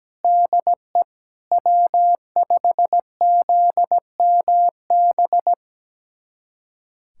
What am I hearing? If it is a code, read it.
Morse code, heard as a keyed tone, about 17 wpm, 700 Hz. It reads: DE W5ZMB
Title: USA Callsign DE Callsign - 5 Characters - 17wpm